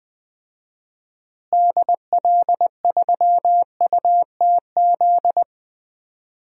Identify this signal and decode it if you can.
Morse code, heard as a keyed tone, about 20 wpm, 700 Hz. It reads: DL3UTZ
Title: DXCC-100 Call Signs - 6 Characters - Repeated 3x - 20wpm